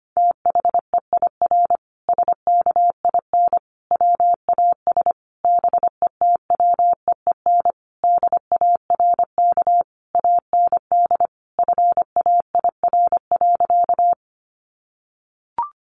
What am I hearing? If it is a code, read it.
Morse code, heard as a keyed tone, about 25 wpm, 700 Hz. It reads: THEIR SKIN WAS BETWEEN DARK AND FAIR.
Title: Sentences from Top 1300 Words 25wpm